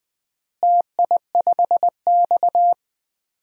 Morse code, heard as a keyed tone, about 19 wpm, 700 Hz. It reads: TI5X
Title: DXCC-100 Call Signs - 4 Characters - Repeated 3x - 20wpm